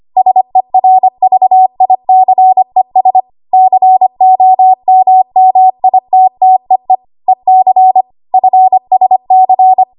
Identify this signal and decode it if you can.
Morse code, heard as a keyed tone, about 25 wpm, 750 Hz. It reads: SERVICES COMMITTEE EC FSC
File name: Archive / 25 WPM / 25WPM